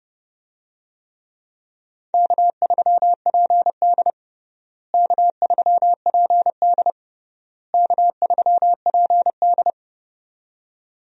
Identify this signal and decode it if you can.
Morse code, heard as a keyed tone, about 30 wpm, 700 Hz. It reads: K3PD K3PD K3PD
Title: USA Call Signs - 4 Characters - Repeated 3x - 30wpm